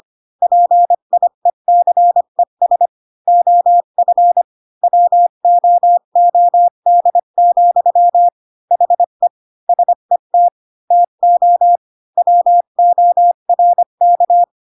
Morse code, heard as a keyed tone, about 25 wpm, 700 Hz. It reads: PIECES OF WOOD, HE SET TO WORK
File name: Pinocchio / Pinocchio ch8 20 25 ik1hge